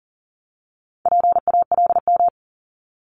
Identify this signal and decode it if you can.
Morse code, heard as a keyed tone, about 38 wpm, 700 Hz. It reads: PALM
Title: Multi-letter Phonemes 40wpm